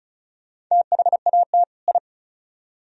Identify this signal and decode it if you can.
Morse code, heard as a keyed tone, about 47 wpm, spaced as for 41 wpm, 700 Hz. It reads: THAT I
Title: Speed-Racing - Sets of 2 Words 35wpm